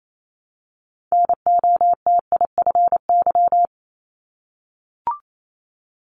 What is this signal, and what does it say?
Morse code, heard as a keyed tone, about 28 wpm, 700 Hz. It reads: NOTIFY
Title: Multi-letter Phonemes Repeated 3x 28wpm